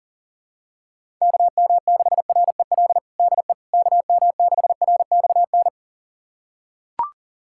Morse code, heard as a keyed tone, about 40 wpm, 700 Hz. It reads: KM6REL DE KM6RXN
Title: USA Callsign DE Callsign - 6 Characters - 40wpm